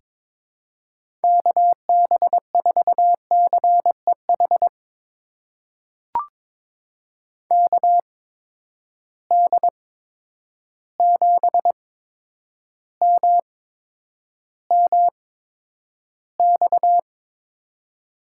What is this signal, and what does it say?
Morse code, heard as a keyed tone, about 22 wpm, spaced as for 9 wpm, 700 Hz. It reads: KB4CEH KD7MMX
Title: USA Call Signs - 6 Characters - 8x Character Spacing - 22wpm